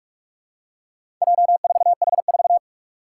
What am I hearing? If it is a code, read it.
Morse code, heard as a keyed tone, about 41 wpm, 700 Hz. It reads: J4H4
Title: Four Challenging Letter-Number Combinations 45wpm